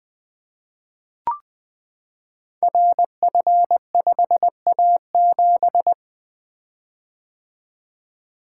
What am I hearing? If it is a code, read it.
Morse code, heard as a keyed tone, about 20 wpm, 700 Hz. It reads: RF5A7
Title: Code Groups - 5 Letter:Numbers - 20wpm